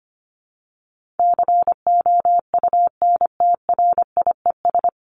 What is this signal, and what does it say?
Morse code, heard as a keyed tone, about 25 wpm, 700 Hz. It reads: COUNTRIES
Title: ES:S Suffix - Top 100 Words with Base Words 25wpm